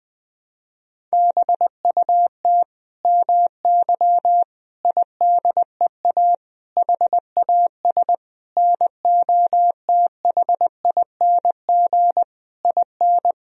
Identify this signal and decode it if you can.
Morse code, heard as a keyed tone, about 20 wpm, 700 Hz. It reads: BUT MY IDEA HAS NOTHING IN
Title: Sentences from Top 1000 Words - Review Part 3 - 20wpm